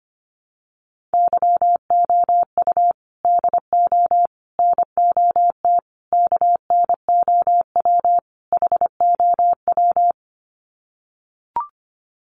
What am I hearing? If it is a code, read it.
Morse code, heard as a keyed tone, about 25 wpm, 700 Hz. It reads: YOU DO NOT KNOW HOW
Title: Sets of 5 Words Encore 25wpm